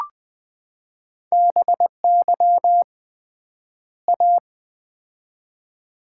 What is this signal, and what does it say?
Morse code, heard as a keyed tone, about 20 wpm, 700 Hz. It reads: BY A
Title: 3x Word-Spacing - Sets of 2 Words - 20wpm